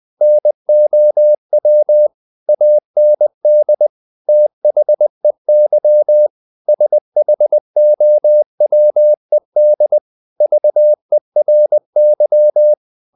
Morse code, heard as a keyed tone, about 20 wpm, 600 Hz. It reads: NOW AND THEY SHOWED VERY
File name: the old man and the sea / the old man and the sea part 1 20wpm